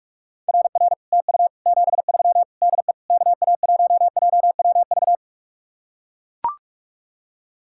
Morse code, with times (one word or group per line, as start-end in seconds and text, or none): 0.5-0.9 RR
1.1-1.5 TU
1.7-2.4 73
2.6-2.9 DE
3.1-5.2 KA1JWV